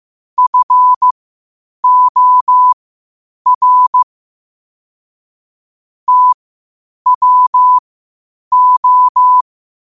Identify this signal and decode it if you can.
Morse code, heard as a keyed tone, about 15 wpm, spaced as for 8 wpm, 1000 Hz. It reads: FOR TWO
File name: qotdfiles / 8wpm